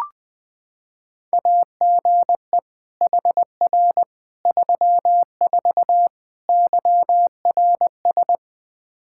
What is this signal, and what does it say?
Morse code, heard as a keyed tone, about 20 wpm, 700 Hz. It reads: AGE HR 34 YRS
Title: K1USN SST QSO Practice 20wpm